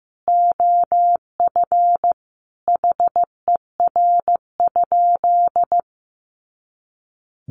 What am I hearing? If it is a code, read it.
Morse code, heard as a keyed tone, about 15 wpm, 700 Hz. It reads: OF HER?
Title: Sentences from Top 200 Words 15wpm